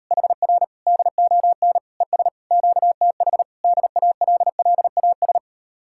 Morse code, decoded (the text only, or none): HR DON ES QTH DALLAS